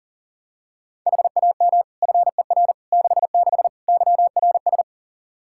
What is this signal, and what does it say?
Morse code, heard as a keyed tone, about 40 wpm, 700 Hz. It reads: HAM FER 66 YRS
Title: Practice QSO - Exchange 3 - More Info - 40wpm